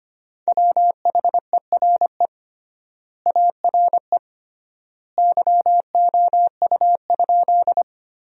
Morse code, heard as a keyed tone, about 25 wpm, 700 Hz. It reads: WHERE ARE YOU?
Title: 3x Word-Spacing - Sentences from Top 200 Words - 25wpm